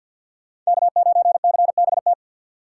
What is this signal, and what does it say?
Morse code, heard as a keyed tone, about 63 wpm, 700 Hz. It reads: K9CBT